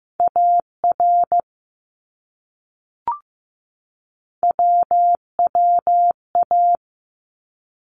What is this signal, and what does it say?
Morse code, heard as a keyed tone, about 15 wpm, 700 Hz. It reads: AR WWA
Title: ARRL Sections 15wpm